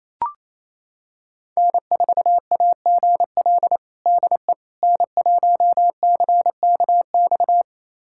Morse code, heard as a keyed tone, about 28 wpm, 700 Hz. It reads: N4AGL DE N1CKX
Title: USA Callsign DE Callsign - 5 Characters - 28wpm